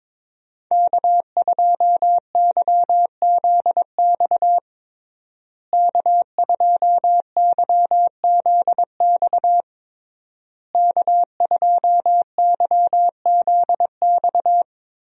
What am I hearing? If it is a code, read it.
Morse code, heard as a keyed tone, about 22 wpm, 700 Hz. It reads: K2YZX K2YZX K2YZX
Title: USA Call Signs - 5 Characters - Repeated 3x - 22wpm